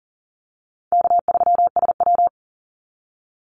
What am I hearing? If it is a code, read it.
Morse code, heard as a keyed tone, about 40 wpm, 700 Hz. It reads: K3SW
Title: Call Signs Repeated 3x - 40wpm